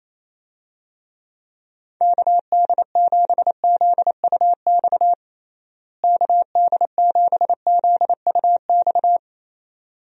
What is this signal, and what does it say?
Morse code, heard as a keyed tone, about 28 wpm, 700 Hz. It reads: KD7ZUX KD7ZUX
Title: USA Call Signs - 6 Characters - Repeated 2x - 28wpm